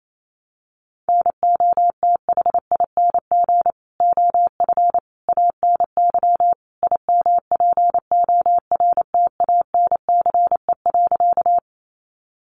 Morse code, heard as a keyed tone, about 28 wpm, 700 Hz. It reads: NOTHING OF ANY IMPORTANCE.
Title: Sentences from Top 1200 Words 28wpm